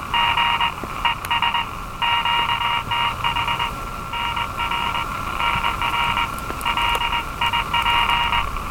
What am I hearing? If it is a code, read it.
Morse code, heard as a keyed tone, about 21 wpm, 950 Hz, with noise. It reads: G ES QTH NR NR LIP
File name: attachments / RV3GM EW1CY